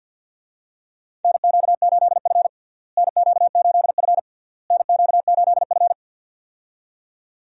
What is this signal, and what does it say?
Morse code, heard as a keyed tone, about 52 wpm, 700 Hz. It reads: NQ8F NQ8F NQ8F
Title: USA Call Signs - 4 Characters - Repeated 3x - 50wpm